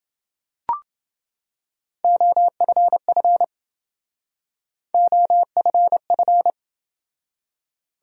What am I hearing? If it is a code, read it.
Morse code, heard as a keyed tone, about 28 wpm, 700 Hz. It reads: OFF OFF